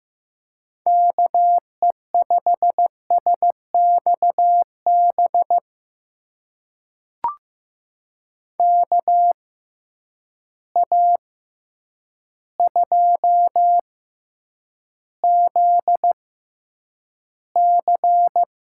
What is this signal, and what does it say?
Morse code, heard as a keyed tone, about 15 wpm, spaced as for 8 wpm, 700 Hz. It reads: KE5SXB KA2ZC
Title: USA Call Signs - 6 Characters - 6x Character Spacing - 15wpm